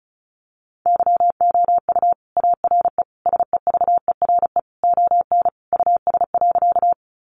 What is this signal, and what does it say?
Morse code, heard as a keyed tone, about 34 wpm, 700 Hz. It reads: YOU ARE SEVERE ON US.
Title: Sentences from Top 1400 Words 35wpm